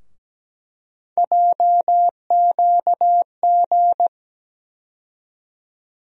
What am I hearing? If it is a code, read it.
Morse code, heard as a keyed tone, about 17 wpm, 700 Hz. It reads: JQG
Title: USA Call Sign Postfixes - 3 Letters 17wpm